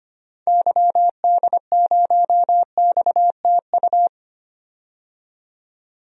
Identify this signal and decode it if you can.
Morse code, heard as a keyed tone, about 25 wpm, 700 Hz. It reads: YD0XTU
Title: DXCC-100 Call Signs - 6 Characters - 25wpm